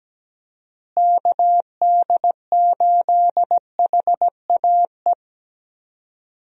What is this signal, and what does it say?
Morse code, heard as a keyed tone, about 17 wpm, 700 Hz. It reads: KD8HAE
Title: USA Call Signs - 6 Characters - Repeated 3x - 17wpm